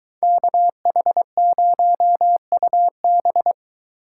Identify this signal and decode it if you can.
Morse code, heard as a keyed tone, about 22 wpm, 700 Hz. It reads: KH0UB